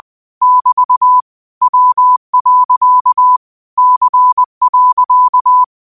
Morse code, heard as a keyed tone, about 20 wpm, 1000 Hz. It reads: <BT> W. C.
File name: 20wpm